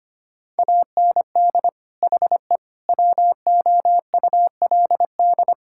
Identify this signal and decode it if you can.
Morse code, heard as a keyed tone, about 25 wpm, 700 Hz. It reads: AND HE WOULD
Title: Sentences from Top 1600 Words 25wpm